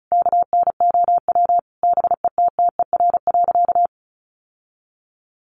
Morse code, heard as a keyed tone, about 34 wpm, 700 Hz. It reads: KNOW BETTER.
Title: Sentences from Top 1400 Words 35wpm